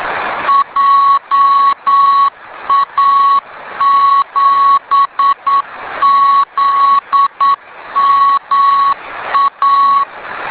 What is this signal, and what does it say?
Morse code, heard as a keyed tone, about 9 wpm, 1110 Hz. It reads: JA7ZMA